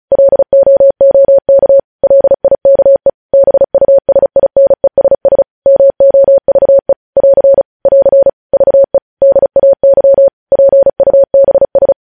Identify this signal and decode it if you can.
Morse code, heard as a keyed tone, about 36 wpm, 550 Hz. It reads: LOOK LIKE BUSINESS MOVE <AR> <AR> VE DAY PUBS
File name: podcast / News headlines-1x-35x35